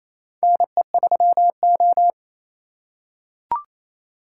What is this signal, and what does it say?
Morse code, heard as a keyed tone, about 28 wpm, 700 Hz. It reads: NE3O
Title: USA Call Signs - 4 Characters - 3x Character Spacing - 28wpm